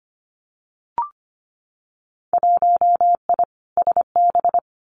Morse code, heard as a keyed tone, about 25 wpm, 700 Hz. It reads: 1I SB
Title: Winter Field Day Exchanges Encore 25wpm